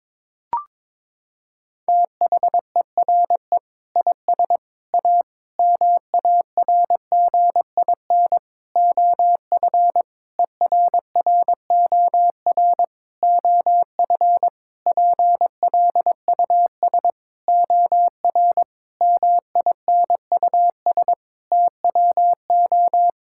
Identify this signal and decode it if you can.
Morse code, heard as a keyed tone, about 22 wpm, 700 Hz. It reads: THERE IS A MARGIN OF ERROR OF PLUS OR MINUS TWO
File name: practice / 22wpm / Sentences with 8-Word Phrases - 22wpm